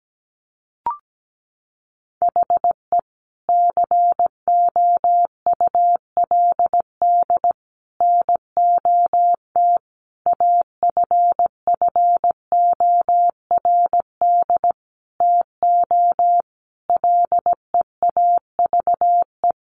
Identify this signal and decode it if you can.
Morse code, heard as a keyed tone, about 17 wpm, 700 Hz. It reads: HE COULD NOT AFFORD TO LEAVE
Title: Sentences from Top 1100 Words 17wpm